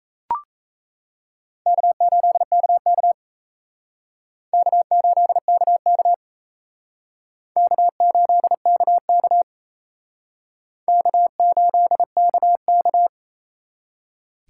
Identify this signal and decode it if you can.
Morse code, heard as a keyed tone, about 35 wpm, 700 Hz. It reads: K8KK K8KK K8KK K8KK